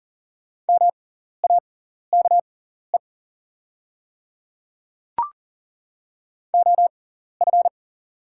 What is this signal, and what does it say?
Morse code, heard as a keyed tone, about 40 wpm, spaced as for 14 wpm, 700 Hz. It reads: MAKE OF
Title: 6x Character Spacing - Top 100 Words - 40wpm